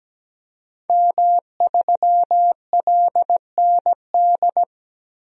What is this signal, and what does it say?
Morse code, heard as a keyed tone, about 17 wpm, 700 Hz. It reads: M3LND
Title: DXCC-100 Call Signs - 5 Characters - Repeated 3x - 17wpm